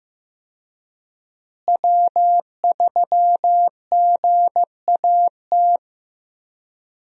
Morse code, heard as a keyed tone, about 15 wpm, 700 Hz. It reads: W3GAT
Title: USA Call Signs - 5 Characters - Repeated 3x - 15wpm